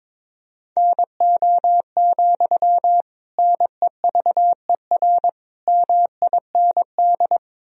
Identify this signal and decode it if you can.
Morse code, heard as a keyed tone, about 22 wpm, 700 Hz. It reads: NO, NEVER MIND
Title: Sentences from Top 1000 Words - Review Part 1 - 22wpm